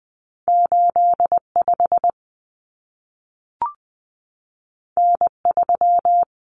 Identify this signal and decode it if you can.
Morse code, heard as a keyed tone, about 20 wpm, 700 Hz. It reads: 85 N3
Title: Common Cut Numbers - 2-Digit 20wpm